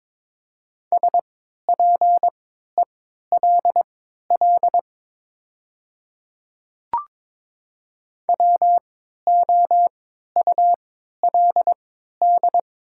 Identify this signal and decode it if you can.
Morse code, heard as a keyed tone, about 23 wpm, spaced as for 13 wpm, 700 Hz. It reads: SPELL WOULD
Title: 3x Character Spacing - Top 200 Words - 22wpm